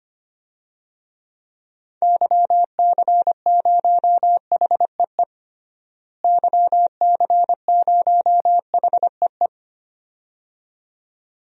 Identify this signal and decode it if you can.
Morse code, heard as a keyed tone, about 25 wpm, 700 Hz. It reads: YC0HEE YC0HEE
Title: DXCC-100 Call Signs - 6 Characters - Repeated 2x - 25wpm